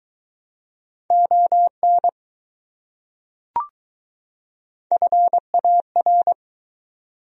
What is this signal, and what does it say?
Morse code, heard as a keyed tone, about 20 wpm, 700 Hz. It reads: ON FAR